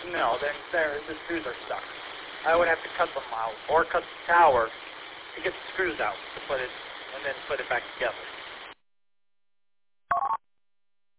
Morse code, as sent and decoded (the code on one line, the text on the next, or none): none